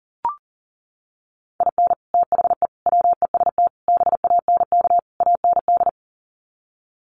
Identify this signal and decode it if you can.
Morse code, heard as a keyed tone, about 40 wpm, 700 Hz. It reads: IN THE WEST BANK AND
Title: Sets of 5 Words Encore2 40wpm